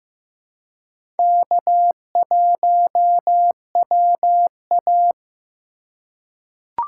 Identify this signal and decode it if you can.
Morse code, heard as a keyed tone, about 15 wpm, 700 Hz. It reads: K1WA
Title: USA Call Signs - 3x Character Spacing - 15wpm